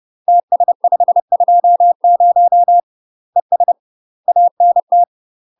Morse code, heard as a keyed tone, about 30 wpm, 700 Hz. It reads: TS520 ES ANT
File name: UK MASTER QSOs TRAINING / 30WPM / UK MASTER QSOs 0034